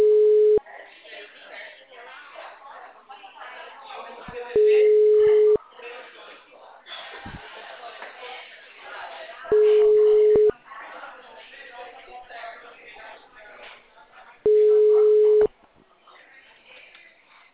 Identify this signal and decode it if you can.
Ringback tone (Europe, 425 Hz), with speech.